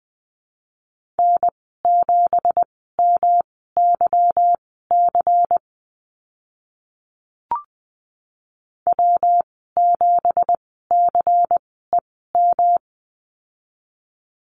Morse code, heard as a keyed tone, about 20 wpm, spaced as for 14 wpm, 700 Hz. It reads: N7MYC W7CEM
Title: 167 Lesson 27g - Number 7 - Call Signs 20wpm